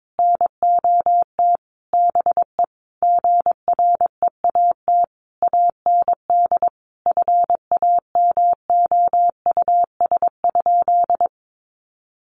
Morse code, heard as a keyed tone, about 22 wpm, 700 Hz. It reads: NOT BE GREAT AND FAMOUS?
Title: Sentences from Top 1500 Words 22wpm